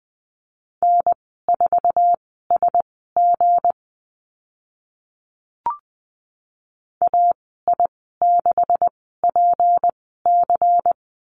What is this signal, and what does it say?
Morse code, heard as a keyed tone, about 20 wpm, spaced as for 14 wpm, 700 Hz. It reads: N4SG AI6PC
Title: 167 Lesson 27g - Number 7 - Call Signs 20wpm